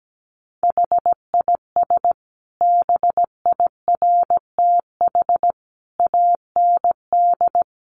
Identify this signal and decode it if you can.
Morse code, heard as a keyed tone, about 17 wpm, 700 Hz. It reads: HIS BIRTH AND
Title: Sentences from Top 900 Words 17wpm